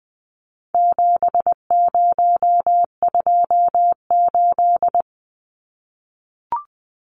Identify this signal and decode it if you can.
Morse code, heard as a keyed tone, about 20 wpm, 700 Hz. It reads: 7028
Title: Numbers - 4-Digit 20wpm